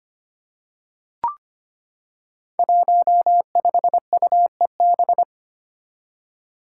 Morse code, heard as a keyed tone, about 25 wpm, 700 Hz. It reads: 15UEB
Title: Five Challenging Letter-Number Combinations 25wpm